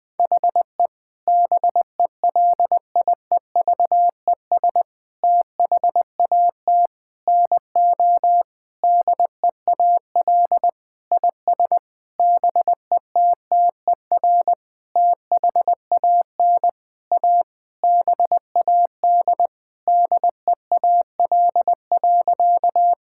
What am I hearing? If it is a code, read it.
Morse code, heard as a keyed tone, about 21 wpm, 700 Hz. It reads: HE BELIEVES THAT NO DEAL IS BETTER THAN A BAD DEAL.
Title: Sentences with 9-Word Phrases - Spoken Beginning - 20wpm